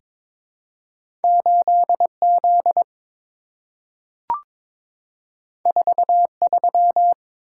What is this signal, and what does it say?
Morse code, heard as a keyed tone, about 22 wpm, 700 Hz. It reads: 8Z 43
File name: practice / 22wpm / Two Letter-Number Combinations - 22wpm